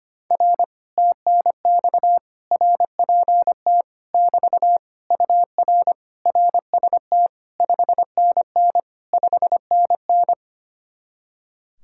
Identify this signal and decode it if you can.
Morse code, heard as a keyed tone, about 24 wpm, 700 Hz. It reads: R TNX RPT <BT> UR RST 5NN 5NN
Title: Speed-Racing Common QSO Phrases - 25wpm